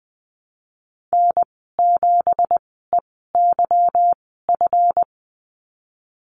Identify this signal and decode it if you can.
Morse code, heard as a keyed tone, about 20 wpm, spaced as for 15 wpm, 700 Hz. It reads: N7EYF
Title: 173 Lesson 28g - Number 9 - Call Signs 20wpm